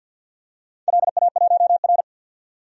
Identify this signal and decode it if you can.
Morse code, heard as a keyed tone, about 50 wpm, 700 Hz. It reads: LA1R
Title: DXCC-100 Call Signs - 4 Characters - 50wpm